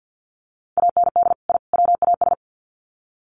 Morse code, heard as a keyed tone, about 49 wpm, 700 Hz. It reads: AND I WAS
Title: Sets of 3 Words 50wpm